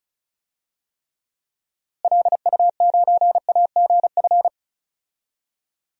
Morse code, heard as a keyed tone, about 36 wpm, 700 Hz. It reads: LU9AGF
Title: DXCC-100 Call Signs - 6 Characters - Repeated 2x - 35wpm